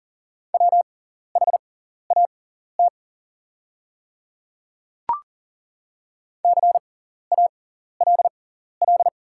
Morse code, heard as a keyed tone, about 42 wpm, spaced as for 14 wpm, 700 Hz. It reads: WHAT CALL
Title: 6x Character Spacing - Top 100 Words - 40wpm